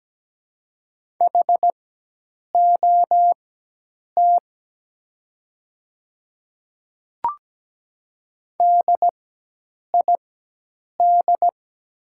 Morse code, heard as a keyed tone, about 17 wpm, spaced as for 8 wpm, 700 Hz. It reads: HOT DID
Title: 4x Character Spacing - Top 100 Words - 17wpm